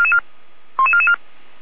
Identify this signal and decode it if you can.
Five-tone selective call (ZVEI), heard as quick runs of five tones, tones of about 70 ms. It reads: 11444